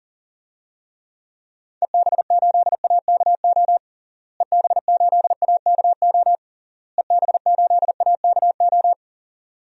Morse code, heard as a keyed tone, about 40 wpm, 700 Hz. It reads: EB8AKO EB8AKO EB8AKO
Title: DXCC-100 Call Signs - 6 Characters - Repeated 3x - 40wpm